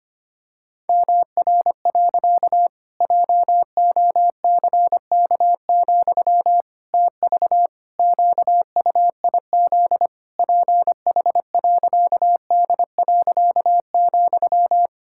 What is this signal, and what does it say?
Morse code, heard as a keyed tone, about 25 wpm, 700 Hz. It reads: MR. JOCK, TV QUIZ PH.D.,
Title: Pangram Challenge v2 - 25wpm